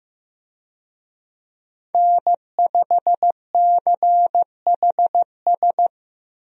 Morse code, heard as a keyed tone, about 15 wpm, 700 Hz. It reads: N5CHS
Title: USA Call Signs - 5 Characters - Repeated 3x - 15wpm